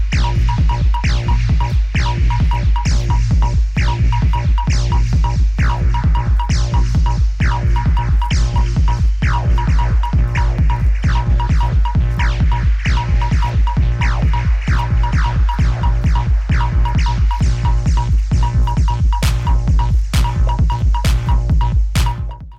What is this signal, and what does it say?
There is no code.